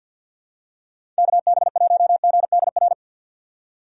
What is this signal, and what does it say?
Morse code, heard as a keyed tone, about 52 wpm, 700 Hz. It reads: KB1GDR
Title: USA Call Signs - 6 Characters - 3x Character Spacing - 50wpm